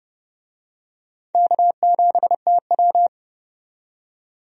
Morse code, heard as a keyed tone, about 30 wpm, 700 Hz. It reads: K7TW
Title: USA Call Signs - 4 Characters - 30wpm